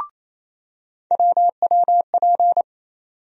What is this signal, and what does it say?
Morse code, heard as a keyed tone, about 28 wpm, 700 Hz. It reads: WWP
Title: USA Call Sign Postfixes - 3 Letters 28wpm